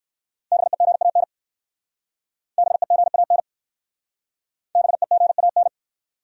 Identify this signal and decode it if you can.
Morse code, heard as a keyed tone, about 58 wpm, 700 Hz. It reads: BEGAN BEGAN BEGAN